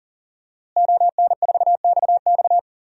Morse code, heard as a keyed tone, about 40 wpm, 700 Hz. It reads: ON4XX
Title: DXCC-100 Call Signs - 5 Characters - Repeated 2x - 40wpm